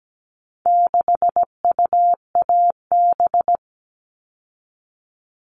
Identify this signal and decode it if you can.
Morse code, heard as a keyed tone, about 17 wpm, 700 Hz. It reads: 6UAB
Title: All Cut Numbers - 4-Digit 17wpm